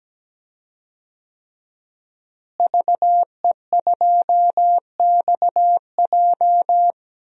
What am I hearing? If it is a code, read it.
Morse code, heard as a keyed tone, about 17 wpm, 700 Hz. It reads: VE2XJ
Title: DXCC-100 Call Signs - 5 Characters - Repeated 3x - 17wpm